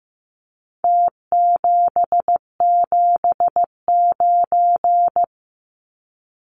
Morse code, heard as a keyed tone, about 15 wpm, 700 Hz. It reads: T779
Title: Numbers - Contest Style - 4-Digit 15wpm